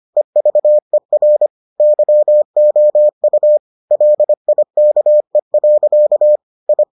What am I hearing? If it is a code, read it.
Morse code, heard as a keyed tone, about 25 wpm, 600 Hz. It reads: EVER YOU LIKE. I